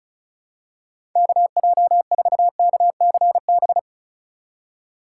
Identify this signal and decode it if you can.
Morse code, heard as a keyed tone, about 36 wpm, 700 Hz. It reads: KJ4KCB
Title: USA Call Signs - 6 Characters - 4x Character Spacing - 35wpm